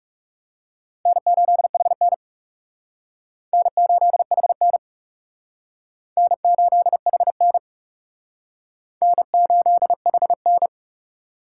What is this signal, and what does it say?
Morse code, heard as a keyed tone, about 37 wpm, 700 Hz. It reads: N8HN N8HN N8HN N8HN
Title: Speed Racing - USA 1x2 Call Signs 30wpm